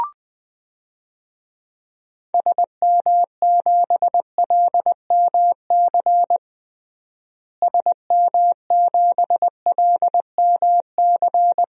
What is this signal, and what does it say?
Morse code, heard as a keyed tone, about 20 wpm, 700 Hz. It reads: SM7LMC SM7LMC
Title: DXCC-100 Call Signs - 6 Characters - Repeated 2x - 20wpm